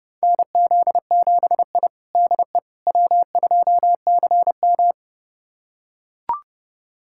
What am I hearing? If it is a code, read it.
Morse code, heard as a keyed tone, about 30 wpm, 700 Hz. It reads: NZ7I DE W2CM
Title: USA Callsign DE Callsign - 4 Characters - 30wpm